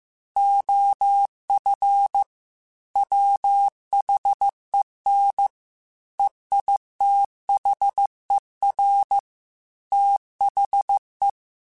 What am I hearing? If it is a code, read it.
Morse code, heard as a keyed tone, about 15 wpm, 800 Hz. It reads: OF WHEN EITHER THE
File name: Technical Subjects / Technical Subjects-mp3s / AND Gate @13